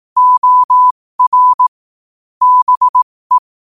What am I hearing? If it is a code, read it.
Morse code, heard as a keyed tone, about 18 wpm, spaced as for 15 wpm, 1000 Hz. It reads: OR BE
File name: qotdfiles / 15wpm